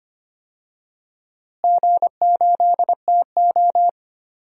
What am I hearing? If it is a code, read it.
Morse code, heard as a keyed tone, about 25 wpm, 700 Hz. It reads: G8TO
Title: DXCC-100 Call Signs - 4 Characters - Repeated 3x - 25wpm